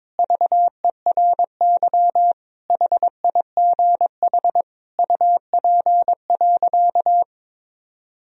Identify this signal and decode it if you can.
Morse code, heard as a keyed tone, about 22 wpm, 700 Hz. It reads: VERY HIGH UP.
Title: Sentences from Top 1000 Words - Review Part 1 - 22wpm